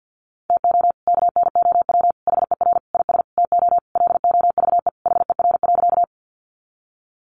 Morse code, heard as a keyed tone, about 50 wpm, 700 Hz. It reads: TO KNOW HER IS TO LOVE HER.
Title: Sentences from Top 400 Words 50wpm